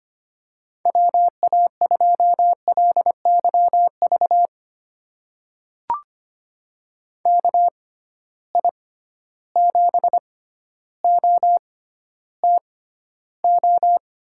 Morse code, heard as a keyed tone, about 25 wpm, spaced as for 13 wpm, 700 Hz. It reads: WA2LYV KI7OTO
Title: USA Call Signs - 6 Characters - 6x Character Spacing - 25wpm